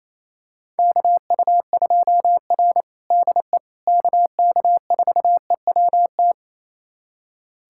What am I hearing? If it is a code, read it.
Morse code, heard as a keyed tone, about 28 wpm, 700 Hz. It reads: KU2R DE KK4EWT
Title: USA Callsign DE Callsign 28wpm